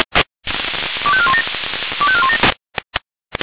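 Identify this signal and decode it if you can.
Five-tone selective call (ZVEI), heard as quick runs of five tones, tones of about 70 ms. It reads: 25417 25417
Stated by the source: ZVEI five-tone selective call received on an SDR.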